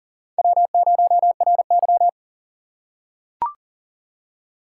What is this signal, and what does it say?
Morse code, heard as a keyed tone, about 40 wpm, 700 Hz. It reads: W0RY
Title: USA Call Signs - 4 Characters - 3x Character Spacing - 40wpm